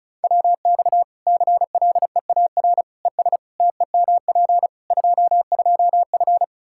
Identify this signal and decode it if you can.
Morse code, heard as a keyed tone, about 34 wpm, 700 Hz. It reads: WX CLEAR ES TEMP 22F
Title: Practice QSO - Exchange 2 - Info - 35wpm